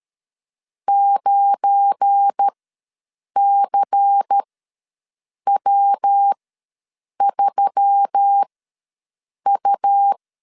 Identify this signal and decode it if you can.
Morse code, heard as a keyed tone, about 13 wpm, spaced as for 7 wpm, 800 Hz. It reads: 9CW3U
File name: LEZIONI / 10 SIGNS / 10C DOT ws 12wpm 3548